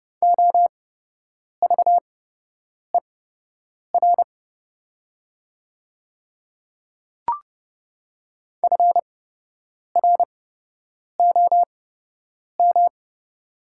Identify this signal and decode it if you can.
Morse code, heard as a keyed tone, about 30 wpm, spaced as for 8 wpm, 700 Hz. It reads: OVER FROM
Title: 8x Character Spacing - Top 100 Words - 30wpm